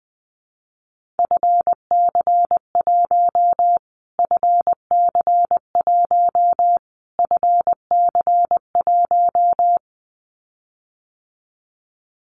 Morse code, heard as a keyed tone, about 20 wpm, 700 Hz. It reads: FC1 FC1 FC1
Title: Callsign Prefixes for Top 25 DXCC - Repeated 3x - 20wpm